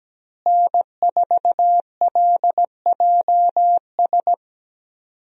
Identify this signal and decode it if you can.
Morse code, heard as a keyed tone, about 17 wpm, 700 Hz. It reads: N4LJS
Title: USA Call Signs - 5 Characters - Repeated 3x - 17wpm